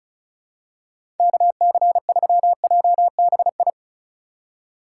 Morse code, heard as a keyed tone, about 36 wpm, 700 Hz. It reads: KC3JBI
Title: USA Call Signs - 3x Character Spacing - 35wpm